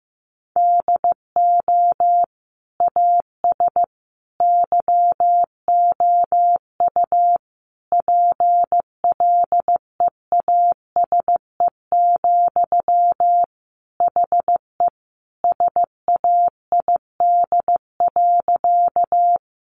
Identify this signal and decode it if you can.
Morse code, heard as a keyed tone, about 15 wpm, 700 Hz. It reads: DO AS YOU PLEASE, HE SAID.
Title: Sentences from Top 1500 Words 15wpm